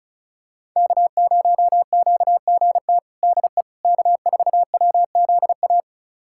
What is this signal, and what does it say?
Morse code, heard as a keyed tone, about 36 wpm, 700 Hz. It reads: K0QGT DE K4WZA
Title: USA Callsign DE Callsign - 5 Characters - 35wpm